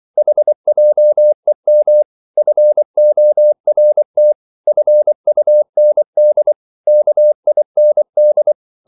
Morse code, heard as a keyed tone, about 24 wpm, 600 Hz. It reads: HJEM FORT FUND KIND
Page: Korte danske ord 24wpm